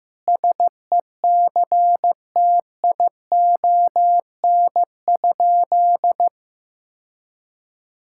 Morse code, heard as a keyed tone, about 15 wpm, 700 Hz. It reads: SECTION?
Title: Sentences from Top 1000 Words - Review Part 5 - 15wpm